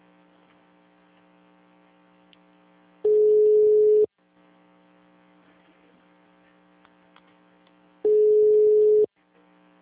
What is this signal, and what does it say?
Ringback tone (Europe, 425 Hz).
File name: out- -2025- - - .12596